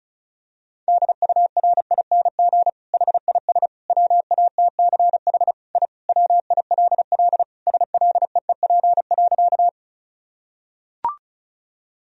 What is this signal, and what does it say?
Morse code, heard as a keyed tone, about 36 wpm, 700 Hz. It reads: DURING HIS WATCH I WILL SLEEP.
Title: Sentences from Top 1000 Words - Review Part 2 - 35wpm